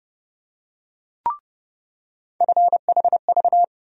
Morse code, heard as a keyed tone, about 30 wpm, 700 Hz. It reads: FHV
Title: USA Call Sign Postfixes - 3 Letters 30wpm